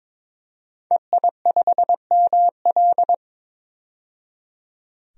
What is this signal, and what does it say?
Morse code, heard as a keyed tone, about 21 wpm, 700 Hz. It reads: EI5ML
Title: DXCC-100 Call Signs - 5 Characters - 22wpm